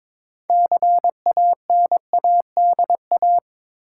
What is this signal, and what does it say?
Morse code, heard as a keyed tone, about 21 wpm, 700 Hz. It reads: CANADA
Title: Sets of 5 Words - New Word Drill - 22wpm